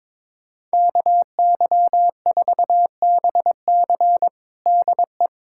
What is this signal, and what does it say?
Morse code, heard as a keyed tone, about 21 wpm, 700 Hz. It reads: KY4BC DE
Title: USA Callsign DE Callsign - 5 Characters - 22wpm